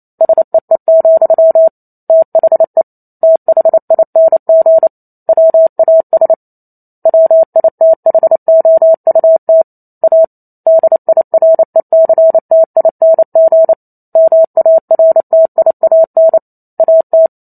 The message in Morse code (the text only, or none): SEE, THE THING WAS WITHOUT A DIRECTING MARTIAN AT